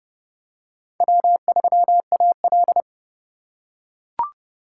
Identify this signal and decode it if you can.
Morse code, heard as a keyed tone, about 32 wpm, 700 Hz. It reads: W3AL
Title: USA Call Signs - 4 Characters - 4x Character Spacing - 30wpm